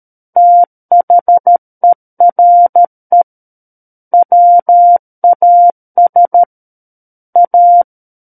Morse code, heard as a keyed tone, about 13 wpm, 700 Hz. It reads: THERE WAS A
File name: Book One / 04-Chapter 4 - 13 WPM